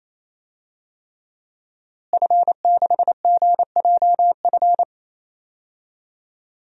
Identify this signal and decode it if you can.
Morse code, heard as a keyed tone, about 28 wpm, 700 Hz. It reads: F6GJF
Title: DXCC-100 Call Signs - 5 Characters - Repeated 3x - 28wpm